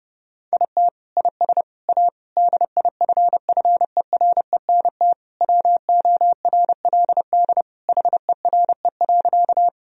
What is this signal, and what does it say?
Morse code, heard as a keyed tone, about 30 wpm, 700 Hz. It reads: IT IS A DIFFERENT WORLD HERE.
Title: Sentences from Top 1000 Words - Review Part 3 - 30wpm